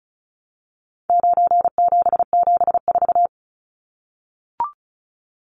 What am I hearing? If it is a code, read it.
Morse code, heard as a keyed tone, about 34 wpm, 700 Hz. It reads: 9774
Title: Numbers - 4-Digit 35wpm